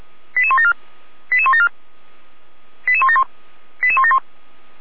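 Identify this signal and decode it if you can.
Five-tone selective call (ZVEI), heard as quick runs of five tones, tones of about 70 ms. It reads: 88174 88174 88171 88171
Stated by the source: ZVEI five-tone selective call received on an SDR by the fire brigade (Feuerwehr) Orlamünde.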